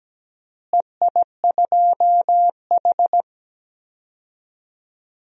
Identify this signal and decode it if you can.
Morse code, heard as a keyed tone, about 17 wpm, 700 Hz. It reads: EI2H
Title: DXCC-100 Call Signs - 4 Characters - Repeated 2x - 17wpm